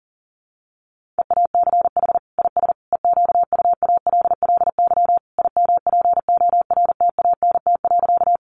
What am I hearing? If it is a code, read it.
Morse code, heard as a keyed tone, about 40 wpm, 700 Hz. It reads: EACH IS EQUALLY IMPORTANT.
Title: Sentences from Top 1400 Words 40wpm